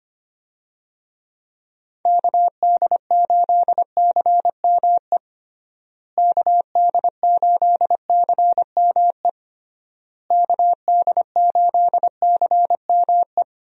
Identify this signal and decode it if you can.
Morse code, heard as a keyed tone, about 25 wpm, 700 Hz. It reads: KD8CME KD8CME KD8CME
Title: USA Call Signs - 6 Characters - Repeated 3x - 25wpm